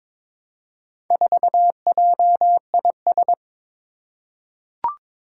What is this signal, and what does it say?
Morse code, heard as a keyed tone, about 22 wpm, 700 Hz. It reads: 4JIS
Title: Four Challenging Letter-Number Combinations 22wpm